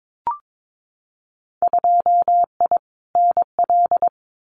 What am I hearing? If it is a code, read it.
Morse code, heard as a keyed tone, about 22 wpm, 700 Hz. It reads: 2I NL
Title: Winter Field Day Exchanges Encore 22wpm